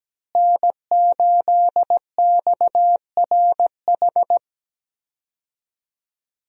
Morse code, heard as a keyed tone, about 17 wpm, 700 Hz. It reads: N8XRH
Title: USA Call Signs - 5 Characters - Repeated 3x - 17wpm